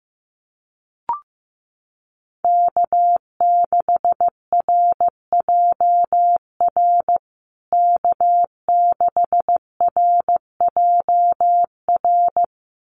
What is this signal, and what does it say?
Morse code, heard as a keyed tone, about 15 wpm, 700 Hz. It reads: K6RJR K6RJR
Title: USA Call Signs Repeated 2x 15wpm